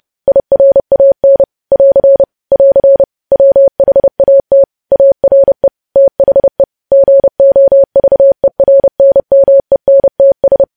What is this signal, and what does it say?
Morse code, heard as a keyed tone, about 30 wpm, 550 Hz. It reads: IRAN <AR> <AR> WHAT ARE THE GOVERNMENTS
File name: podcast / News headlines-1x-30x30